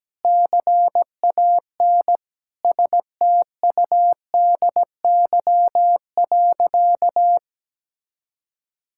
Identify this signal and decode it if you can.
Morse code, heard as a keyed tone, about 17 wpm, 700 Hz. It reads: CAN STUDY.
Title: Sentences from Top 1000 Words - Review Part 5 - 17wpm